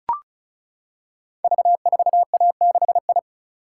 Morse code, heard as a keyed tone, about 36 wpm, 700 Hz. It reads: V4A6I
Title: Code Groups - 5 Letter:Numbers - 35wpm